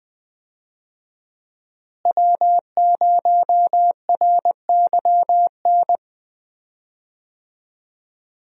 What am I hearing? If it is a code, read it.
Morse code, heard as a keyed tone, about 20 wpm, 700 Hz. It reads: W0RYN
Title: USA Call Signs - 5 Characters - Repeated 3x - 20wpm